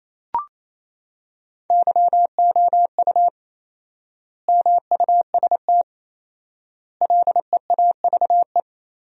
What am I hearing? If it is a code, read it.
Morse code, heard as a keyed tone, about 27 wpm, 700 Hz. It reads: YOU MUST LEAVE
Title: 4x Word-Spacing - Sentences from Top 400 Words - 28wpm